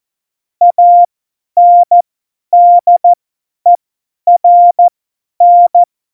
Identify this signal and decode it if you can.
Morse code, heard as a keyed tone, about 14 wpm, spaced as for 10 wpm, 700 Hz. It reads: ANDERN
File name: Morsekurs / Witze / Witz 11. September 2018, 50 F70